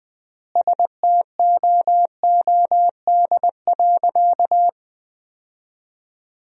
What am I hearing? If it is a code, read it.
Morse code, heard as a keyed tone, about 20 wpm, 700 Hz. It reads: STOOD.
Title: 3x Word-Spacing - Sentences from Top 500 Words - 20wpm